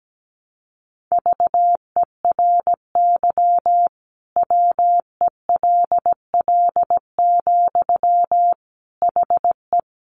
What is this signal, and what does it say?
Morse code, heard as a keyed tone, about 17 wpm, 700 Hz. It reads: VERY WELL, HE
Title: Sentences from Top 200 Words 17wpm